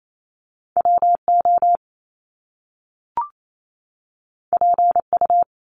Morse code, heard as a keyed tone, about 28 wpm, 700 Hz. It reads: WO PU
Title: Top Two Letter Combinations in Words 28wpm